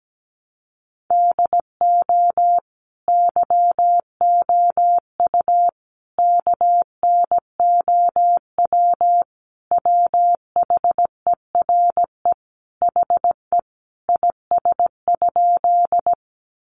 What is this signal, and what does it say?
Morse code, heard as a keyed tone, about 17 wpm, 700 Hz. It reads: DO YOU KNOW WHERE HE IS?
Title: Sentences from Top 300 Words 17wpm